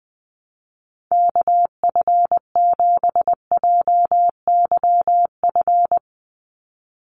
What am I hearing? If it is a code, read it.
Morse code, heard as a keyed tone, about 20 wpm, 700 Hz. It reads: KF7JYF
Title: USA Call Signs 20wpm